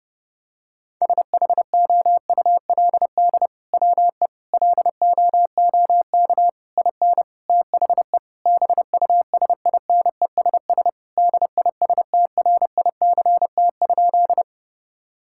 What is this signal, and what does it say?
Morse code, heard as a keyed tone, about 32 wpm, 700 Hz. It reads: SHOULD WE LOOK IN THE BUSINESS DISTRICT?
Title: Sentences from Top 1600 Words 30wpm